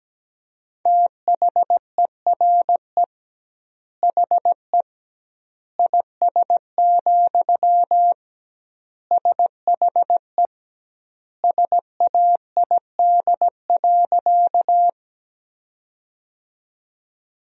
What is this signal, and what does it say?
Morse code, heard as a keyed tone, about 17 wpm, 700 Hz. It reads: THERE HE IS, SHE SAID.
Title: 2x Word-Spacing - Sentences from Top 100 Words - 17wpm